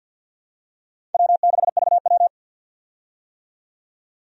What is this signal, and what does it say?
Morse code, heard as a keyed tone, about 52 wpm, 700 Hz. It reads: W6VW
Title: USA Call Signs - 4 Characters - 50wpm